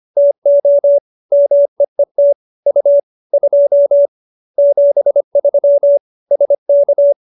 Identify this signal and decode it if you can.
Morse code, heard as a keyed tone, about 25 wpm, 570 Hz. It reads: TO MEET U 2 73 SK